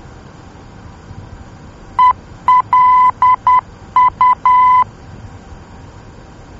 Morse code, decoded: ELU